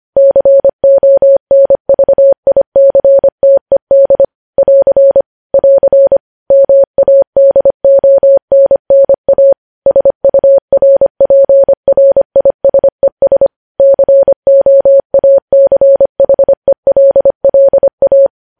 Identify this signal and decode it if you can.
Morse code, heard as a keyed tone, about 25 wpm, 550 Hz. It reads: CONVICTED <AR> <AR> MADONNA SURPRISES COACHELLA
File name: podcast / News headlines-1x-25x25